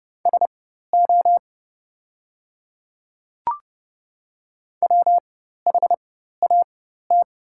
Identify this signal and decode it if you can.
Morse code, heard as a keyed tone, about 29 wpm, spaced as for 14 wpm, 700 Hz. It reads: SO WHAT
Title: 4x Character Spacing - Top 300 Words - 30wpm